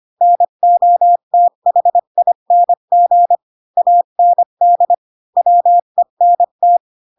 Morse code, heard as a keyed tone, about 25 wpm, 700 Hz. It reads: NOTHING AND WENT
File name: Pinocchio / Pinocchio ch3 20 25 ik1hge